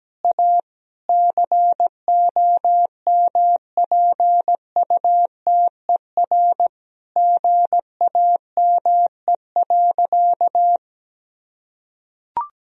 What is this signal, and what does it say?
Morse code, heard as a keyed tone, about 17 wpm, 700 Hz. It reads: A COMPUTER GAME.
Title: Sentences from Top 1000 Words - Review Part 3 - 17wpm